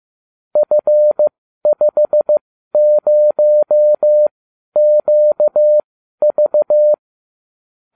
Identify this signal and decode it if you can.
Morse code, heard as a keyed tone, about 15 wpm, spaced as for 12 wpm, 600 Hz. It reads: F50QV